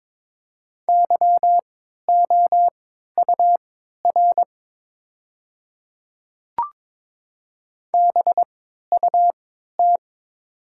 Morse code, heard as a keyed tone, about 22 wpm, spaced as for 12 wpm, 690 Hz. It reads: YOUR BUT
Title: 3x Character Spacing - Top 200 Words - 22wpm